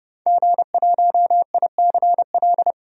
Morse code, heard as a keyed tone, about 30 wpm, 700 Hz. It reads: G1ICL
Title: DXCC-100 Call Signs - 5 Characters - 30wpm